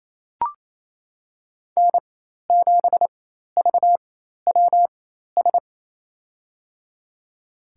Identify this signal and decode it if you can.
Morse code, heard as a keyed tone, about 27 wpm, spaced as for 13 wpm, 700 Hz. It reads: N7VWS
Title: USA Call Signs - 5 Characters - 4x Character Spacing - 28wpm